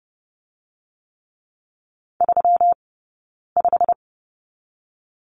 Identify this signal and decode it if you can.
Morse code, heard as a keyed tone, about 30 wpm, spaced as for 9 wpm, 700 Hz. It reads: 35
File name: Lesson 20d - Number 3 - Two Characters 30wpm